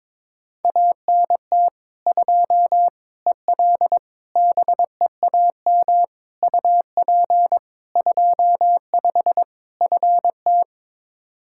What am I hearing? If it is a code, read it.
Morse code, heard as a keyed tone, about 22 wpm, 700 Hz. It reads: ANT 2 EL BEAM UP 25 FT
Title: K1USN SST QSO Practice 22wpm